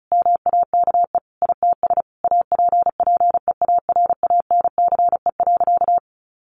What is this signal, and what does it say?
Morse code, heard as a keyed tone, about 34 wpm, 700 Hz. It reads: MAKE ITS APPEARANCE.
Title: Sentences from Top 1100 Words 35wpm